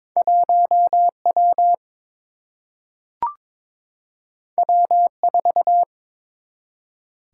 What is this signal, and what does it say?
Morse code, heard as a keyed tone, about 22 wpm, 700 Hz. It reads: 1 W W 4
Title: Single-Letter Single-Number 22wpm